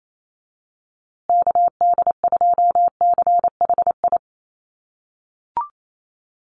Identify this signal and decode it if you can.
Morse code, heard as a keyed tone, about 28 wpm, 700 Hz. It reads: KD2CHI
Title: USA Call Signs Repeated 2x 28wpm